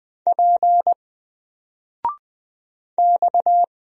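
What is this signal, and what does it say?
Morse code, heard as a keyed tone, about 20 wpm, 700 Hz. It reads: P X
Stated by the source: Single letter and number Morse drill at 20 wpm, from N4LCD.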